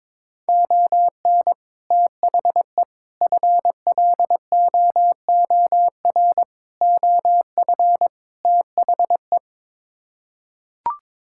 Morse code, heard as a keyed tone, about 21 wpm, 700 Hz. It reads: ON THE FLOOR OF THE
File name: practice / 22wpm / Build A Phrase - 5-Word Phrases - 22wpm